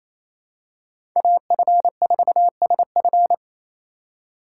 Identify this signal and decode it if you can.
Morse code, heard as a keyed tone, about 28 wpm, 700 Hz. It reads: AF4SF
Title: USA Call Signs - 2x Character Spacing - 28wpm